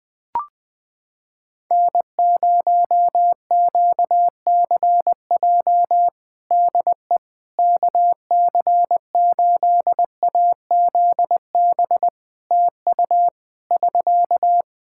Morse code, heard as a keyed tone, about 20 wpm, 700 Hz. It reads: N0QCJ DE KC8AZB TU <SK>
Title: Practice QSO - Exchange 4 - Ending - 20wpm